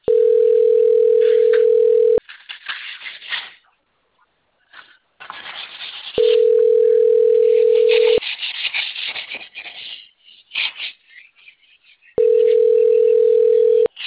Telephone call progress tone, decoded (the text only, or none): ringback tone